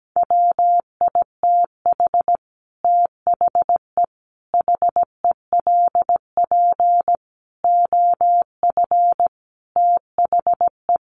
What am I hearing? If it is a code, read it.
Morse code, heard as a keyed tone, about 17 wpm, 700 Hz. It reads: WITH THE HELP OF THE
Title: Sets of 5 Words Encore 17wpm